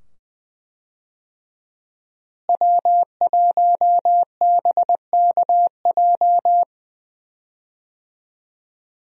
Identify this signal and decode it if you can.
Morse code, heard as a keyed tone, about 20 wpm, 700 Hz. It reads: W1BKJ
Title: USA Call Signs - 5 Characters - Repeated 2x - 20wpm